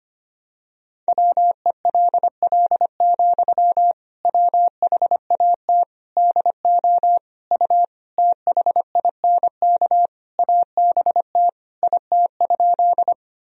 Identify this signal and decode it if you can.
Morse code, heard as a keyed tone, about 25 wpm, 700 Hz. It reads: WELL, WHAT DO U THINK ABT IT?
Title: Sentences from Top 200 Words and Abbreviations 25wpm